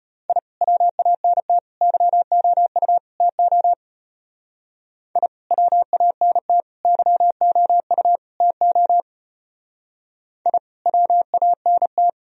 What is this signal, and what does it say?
Morse code, heard as a keyed tone, about 32 wpm, 690 Hz. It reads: I WANT YOU TO I WANT YOU TO I WANT